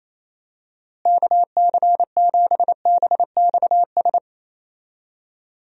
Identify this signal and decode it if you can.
Morse code, heard as a keyed tone, about 28 wpm, 700 Hz. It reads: KC7BXS
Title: USA Call Signs - 6 Characters - Repeated 3x - 28wpm